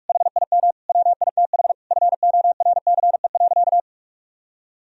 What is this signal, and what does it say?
Morse code, heard as a keyed tone, about 46 wpm, 700 Hz. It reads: HIM WITH FORCE.
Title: Sentences from Top 1000 Words - Review Part 3 - 45wpm